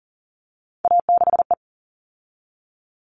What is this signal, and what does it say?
Morse code, heard as a keyed tone, about 40 wpm, 700 Hz. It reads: A6E
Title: All Cut Numbers - 3-Digit 40wpm